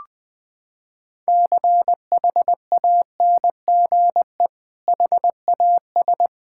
Morse code, heard as a keyed tone, about 19 wpm, 700 Hz. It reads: CHANGE HAS
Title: Sentences from Top 1600 Words 20wpm